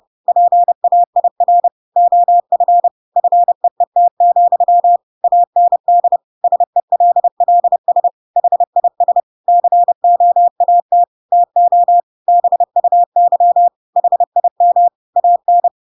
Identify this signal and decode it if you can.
Morse code, heard as a keyed tone, about 30 wpm, 700 Hz. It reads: PAIR OF FEET, AND SELLS HIS COAT TO BUY HIM AN
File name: Pinocchio / Pinocchio ch8 30 30 ik1hge